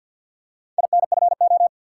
Morse code, heard as a keyed tone, about 53 wpm, 700 Hz. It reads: INFO
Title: Top 180 QSO Elements 50wpm